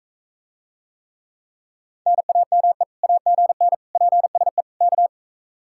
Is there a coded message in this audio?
Yes: Morse code from 2.1 to 5.1 s, heard as a keyed tone, about 41 wpm, 700 Hz. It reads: NAME AGN PSE K